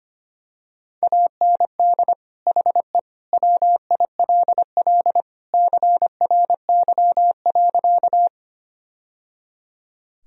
Morse code, heard as a keyed tone, about 25 wpm, 700 Hz. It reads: AND HE WILL CRY.
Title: Sentences from Top 1000 Words - Review Part 2 - 25wpm